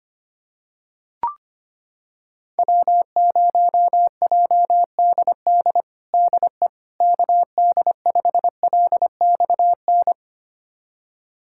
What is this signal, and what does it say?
Morse code, heard as a keyed tone, about 25 wpm, 700 Hz. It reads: W0JDD DE KD5LXN
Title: USA Callsign DE Callsign 25wpm